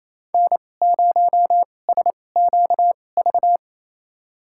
Morse code, heard as a keyed tone, about 28 wpm, spaced as for 20 wpm, 700 Hz. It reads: N0SQV
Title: USA Call Signs - 2x Character Spacing - 28wpm